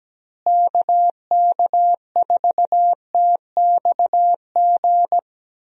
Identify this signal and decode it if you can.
Morse code, heard as a keyed tone, about 17 wpm, 700 Hz. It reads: KK4TXG
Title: USA Call Signs - 6 Characters - Repeated 2x - 17wpm